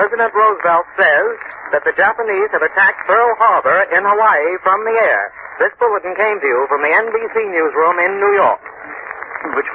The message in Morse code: none